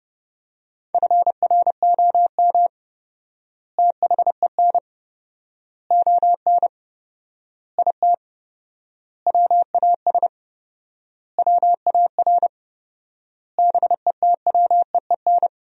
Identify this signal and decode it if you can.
Morse code, heard as a keyed tone, about 29 wpm, 700 Hz. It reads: FROM THEN ON IT WAS WAR BETWEEN
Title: 4x Word-Spacing - Sentences from Top 400 Words - 30wpm